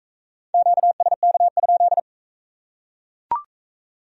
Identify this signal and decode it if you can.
Morse code, heard as a keyed tone, about 39 wpm, 700 Hz. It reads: QSK?